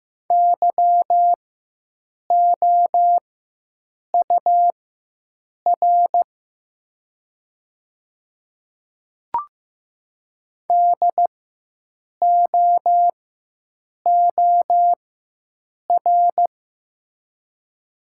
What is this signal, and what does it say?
Morse code, heard as a keyed tone, about 15 wpm, spaced as for 7 wpm, 700 Hz. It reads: YOUR DOOR
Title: 4x Character Spacing - Top 300 Words - 15wpm